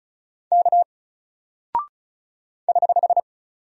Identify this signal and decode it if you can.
Morse code, heard as a keyed tone, about 36 wpm, 700 Hz. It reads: K <HH>
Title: Single Character (Letter:Number:Punctuation:Prosign) 35wpm